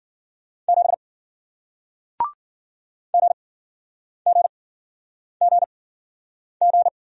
Morse code, heard as a keyed tone, about 53 wpm, spaced as for 9 wpm, 700 Hz. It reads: 7 GGGG